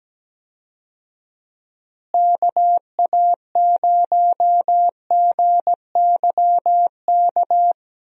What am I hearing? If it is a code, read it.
Morse code, heard as a keyed tone, about 17 wpm, 700 Hz. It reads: KA0GYK
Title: USA Call Signs - 6 Characters - Repeated 2x - 17wpm